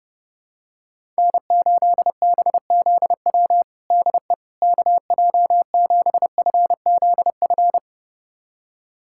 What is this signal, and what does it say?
Morse code, heard as a keyed tone, about 30 wpm, 700 Hz. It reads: N8BZW DE KJ7FZF
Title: USA Callsign DE Callsign 30wpm